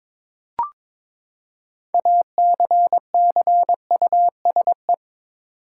Morse code, heard as a keyed tone, about 22 wpm, 700 Hz. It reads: ACCUSE
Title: Six Letter Words - 22wpm